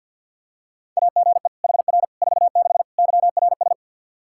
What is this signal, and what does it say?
Morse code, heard as a keyed tone, about 48 wpm, 700 Hz. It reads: AGE HR 46 YRS
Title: K1USN SST QSO Practice 50wpm